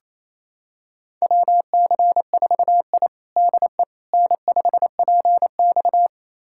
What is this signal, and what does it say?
Morse code, heard as a keyed tone, about 28 wpm, 700 Hz. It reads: WC4I DE N5PX
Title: USA Callsign DE Callsign - 4 Characters - 28wpm